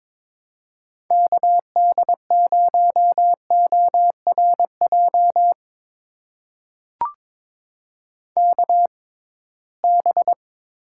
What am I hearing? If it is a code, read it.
Morse code, heard as a keyed tone, about 22 wpm, spaced as for 17 wpm, 700 Hz. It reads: KD0ORJ KB
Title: USA Call Signs - 6 Characters - 6x Character Spacing - 22wpm